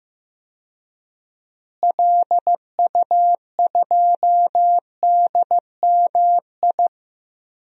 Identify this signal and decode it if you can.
Morse code, heard as a keyed tone, about 15 wpm, 700 Hz. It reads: LU2DMI
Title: DXCC-100 Call Signs - 6 Characters - Repeated 3x - 15wpm